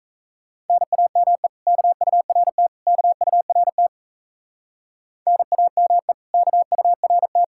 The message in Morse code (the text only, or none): NAME KURT KURT NAME KURT